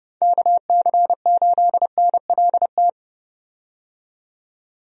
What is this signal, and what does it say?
Morse code, heard as a keyed tone, about 30 wpm, 700 Hz. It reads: KC8NLT
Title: USA Call Signs - 6 Characters - 30wpm